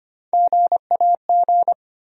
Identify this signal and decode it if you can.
Morse code, heard as a keyed tone, about 25 wpm, 700 Hz. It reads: GAG
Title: USA Call Sign Postfixes - 3 Letters 25wpm